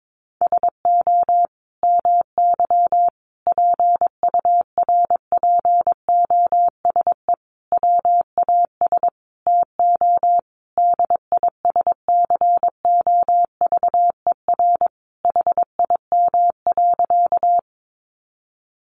Morse code, heard as a keyed tone, about 22 wpm, 700 Hz. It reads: SO MY PURPOSE WAS TO DISCOVER HIM.
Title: Sentences from Top 1100 Words 22wpm